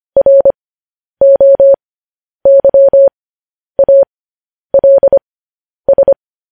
Morse code, heard as a keyed tone, about 25 wpm, spaced as for 10 wpm, 550 Hz. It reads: ROYALS
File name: podcast / News headlines-1x-25x10